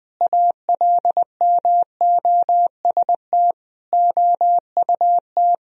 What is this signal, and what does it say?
Morse code, heard as a keyed tone, about 20 wpm, 700 Hz. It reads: ALMOST OUT
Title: Sentences from Top 1000 Words - Review Part 4 - 20wpm